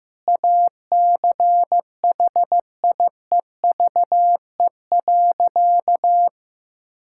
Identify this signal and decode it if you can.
Morse code, heard as a keyed tone, about 15 wpm, 700 Hz. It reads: ACHIEVE.
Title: Sentences from Top 1000 Words - Review Part 6 - 15wpm